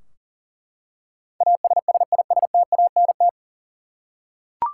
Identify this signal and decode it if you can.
Morse code, heard as a keyed tone, about 40 wpm, 700 Hz. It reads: ASSISTANT
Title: Practice QSO - Exchange 3 - More Info - 40wpm